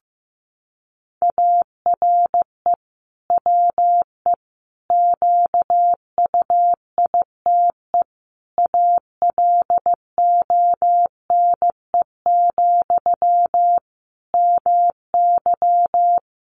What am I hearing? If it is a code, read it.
Morse code, heard as a keyed tone, about 15 wpm, 700 Hz. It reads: ARE WE QUITE ALONE, MY
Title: Sentences from Top 1000 Words 15wpm